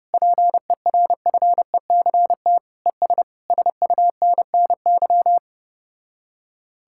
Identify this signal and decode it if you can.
Morse code, heard as a keyed tone, about 33 wpm, 700 Hz. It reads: PERFECT ES SUNNY